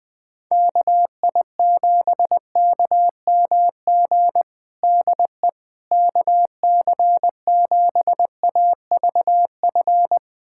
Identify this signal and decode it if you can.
Morse code, heard as a keyed tone, about 20 wpm, 700 Hz. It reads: KI7KMG DE KC7AVF
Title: USA Callsign DE Callsign - 6 Characters - 20wpm